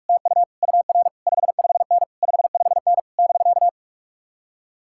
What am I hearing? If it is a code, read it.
Morse code, heard as a keyed tone, about 62 wpm, spaced as for 53 wpm, 700 Hz. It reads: TU UR 55N 55N <BK>
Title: Speed-Racing Common QSO Phrases - 45wpm